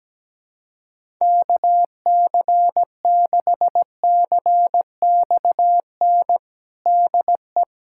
Morse code, heard as a keyed tone, about 17 wpm, 700 Hz. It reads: KC6CXN DE
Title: USA Callsign DE Callsign - 6 Characters - 17wpm